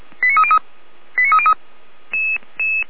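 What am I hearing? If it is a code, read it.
Five-tone selective call (ZVEI), heard as quick runs of five tones, tones of about 80 ms. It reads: 89303
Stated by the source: ZVEI five-tone selective call 89290 received on an SDR.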